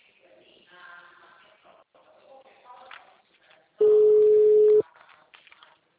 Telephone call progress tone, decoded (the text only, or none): ringback tone